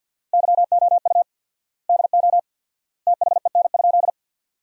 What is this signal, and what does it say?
Morse code, heard as a keyed tone, about 48 wpm, 700 Hz. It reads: YOU DO THEN?
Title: 4x Word-Spacing - Sentences from Top 100 Words - 50wpm